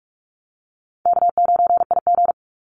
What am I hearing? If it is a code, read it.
Morse code, heard as a keyed tone, about 44 wpm, 700 Hz. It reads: K9IG
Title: Call Signs Repeated 2x - 45wpm